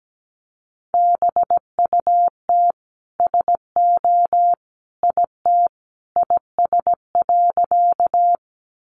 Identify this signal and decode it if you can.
Morse code, heard as a keyed tone, about 17 wpm, 700 Hz. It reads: BUT SO IT IS.
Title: Sentences from Top 100 Words 17wpm